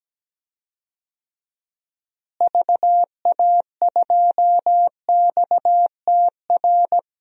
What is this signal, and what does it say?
Morse code, heard as a keyed tone, about 17 wpm, 700 Hz. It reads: VA2XTR
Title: DXCC-100 Call Signs - 6 Characters - Repeated 3x - 17wpm